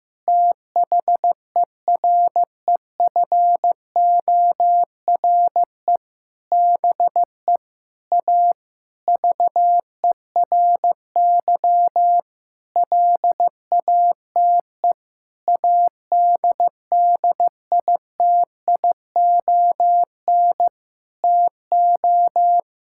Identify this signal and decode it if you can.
Morse code, heard as a keyed tone, about 15 wpm, 700 Hz. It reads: THEREFORE BE A VERY LATE ADDITION TO
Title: Sentences from Top 1000 Words - Review Part 6 - 15wpm